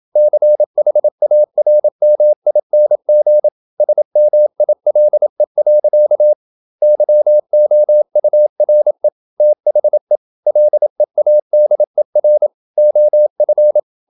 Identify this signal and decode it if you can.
Morse code, heard as a keyed tone, about 27 wpm, 600 Hz. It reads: CHARMING SMILE. YOURE THE LEADER OF
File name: Never 0027wpm